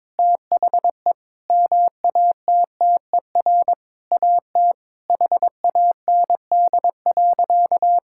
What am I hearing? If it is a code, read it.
Morse code, heard as a keyed tone, about 22 wpm, 700 Hz. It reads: THE MATTER AT HAND.
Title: Sentences from Top 1600 Words 22wpm